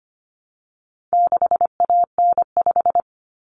Morse code, heard as a keyed tone, about 25 wpm, 700 Hz. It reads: 6AN5
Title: Common Cut Numbers - 4-Digit 25wpm